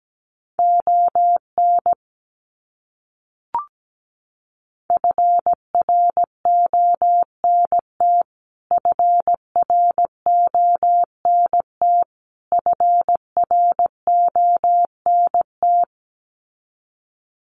Morse code, heard as a keyed tone, about 17 wpm, 700 Hz. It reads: ON FRONT FRONT FRONT
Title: Top 500 Words Repeated 17wpm